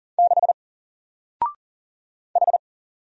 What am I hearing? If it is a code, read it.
Morse code, heard as a keyed tone, about 40 wpm, 700 Hz. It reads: 6 H
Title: Single Letter-Number Alt 40wpm